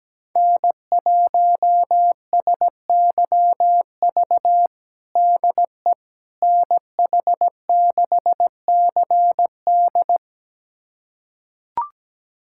Morse code, heard as a keyed tone, about 17 wpm, 700 Hz. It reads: N1SYV DE NH6CD
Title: USA Callsign DE Callsign - 5 Characters - 17wpm